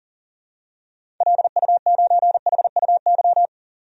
Morse code, heard as a keyed tone, about 40 wpm, 700 Hz. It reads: LU9HUY
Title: DXCC-100 Call Signs - 6 Characters - 40wpm